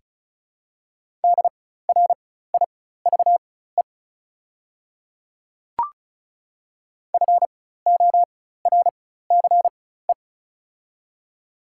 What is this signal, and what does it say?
Morse code, heard as a keyed tone, about 34 wpm, spaced as for 16 wpm, 700 Hz. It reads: DRIVE FORCE
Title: 4x Character Spacing - Top 500 Words - 35wpm